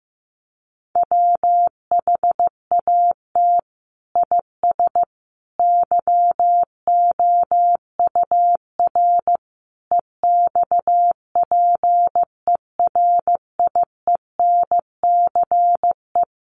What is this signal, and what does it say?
Morse code, heard as a keyed tone, about 15 wpm, 700 Hz. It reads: WHAT IS YOUR EXPERIENCE
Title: Sentences from Top 600 Words 15wpm